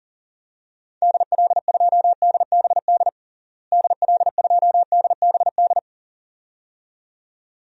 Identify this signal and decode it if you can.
Morse code, heard as a keyed tone, about 40 wpm, 700 Hz. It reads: DL2DBD DL2DBD
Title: DXCC-100 Call Signs - 6 Characters - Repeated 2x - 40wpm